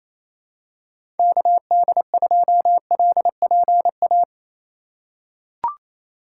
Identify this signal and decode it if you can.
Morse code, heard as a keyed tone, about 28 wpm, 700 Hz. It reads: KD2LPA
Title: USA Call Signs - 6 Characters - 2x Character Spacing - 28wpm